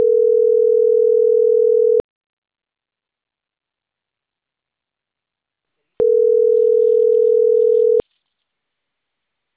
Ringback tone (North America), 440 + 480 Hz.
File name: CENTRAL CAEA / out agcaea01 - .79624